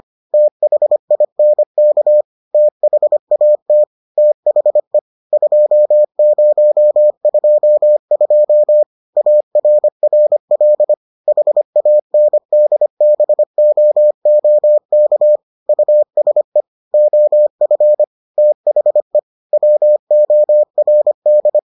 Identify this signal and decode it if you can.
Morse code, heard as a keyed tone, about 25 wpm, 600 Hz. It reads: THINK THAT THE 2022 ARRL HANDBOOK USE OF THE WORD